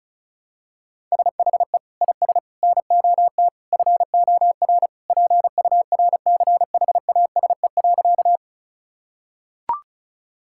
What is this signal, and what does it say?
Morse code, heard as a keyed tone, about 34 wpm, 700 Hz. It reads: SHE IS NOT FOR PURCHASE.
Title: Sentences from Top 1000 Words - Review Part 5 - 35wpm